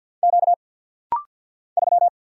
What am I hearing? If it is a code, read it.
Morse code, heard as a keyed tone, about 50 wpm, 700 Hz. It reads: Q 3
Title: Single Letter-Number Alt - Rapid-Fire 50wpm